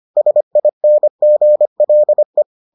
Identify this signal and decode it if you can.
Morse code, heard as a keyed tone, about 25 wpm, 600 Hz. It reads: SINGLE